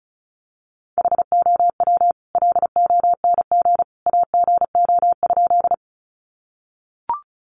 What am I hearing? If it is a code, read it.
Morse code, heard as a keyed tone, about 34 wpm, 700 Hz. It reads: HOW LONG AGO?
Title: Sentences from Top 500 Words 35wpm